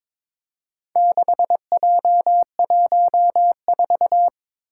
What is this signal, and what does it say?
Morse code, heard as a keyed tone, about 21 wpm, 700 Hz. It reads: 6J14
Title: Four Challenging Letter-Number Combinations 22wpm